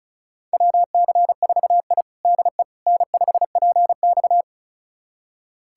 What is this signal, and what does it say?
Morse code, heard as a keyed tone, about 36 wpm, 700 Hz. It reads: WC4I DE N5PX
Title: USA Callsign DE Callsign - 4 Characters - 35wpm